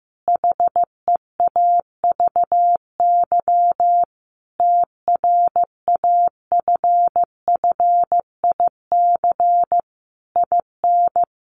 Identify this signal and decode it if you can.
Morse code, heard as a keyed tone, about 15 wpm, 700 Hz. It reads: HEAVY TRAFFIC IN
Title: Sentences from Top 1300 Words 15wpm